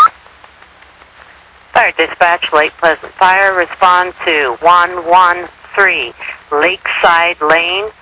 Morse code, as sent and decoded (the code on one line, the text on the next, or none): none